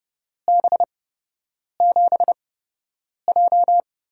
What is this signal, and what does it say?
Morse code, heard as a keyed tone, about 30 wpm, spaced as for 8 wpm, 700 Hz. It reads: B7J
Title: USA Call Signs - 4 Characters - 8x Character Spacing - 30wpm